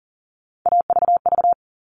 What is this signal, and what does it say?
Morse code, heard as a keyed tone, about 40 wpm, 700 Hz. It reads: AVV
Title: All Cut Numbers - 3-Digit 40wpm